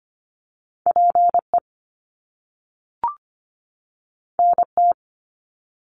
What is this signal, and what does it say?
Morse code, heard as a keyed tone, about 25 wpm, 700 Hz. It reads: PE NT
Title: Canadian Provinces and Territories 25wpm